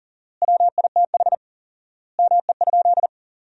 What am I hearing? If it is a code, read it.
Morse code, heard as a keyed tone, about 37 wpm, 700 Hz. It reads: WITH ME?
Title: 4x Word-Spacing - Sentences from Top 300 Words - 40wpm